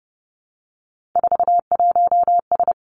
Morse code, heard as a keyed tone, about 32 wpm, 700 Hz. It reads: 41S
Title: Three Challenging Letter-Number Combinations 30wpm